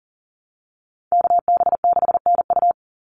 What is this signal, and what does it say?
Morse code, heard as a keyed tone, about 40 wpm, 700 Hz. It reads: KB6NU
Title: Call Signs Repeated 3x - 40wpm